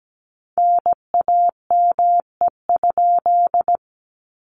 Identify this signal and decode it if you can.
Morse code, heard as a keyed tone, about 17 wpm, 700 Hz. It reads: NAME?
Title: Common QSO Phrases Encore 17wpm